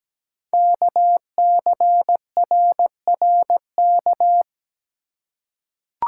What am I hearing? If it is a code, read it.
Morse code, heard as a keyed tone, about 17 wpm, 700 Hz. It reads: KCRRK
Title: Code Groups - 5 Letter:Numbers - 17wpm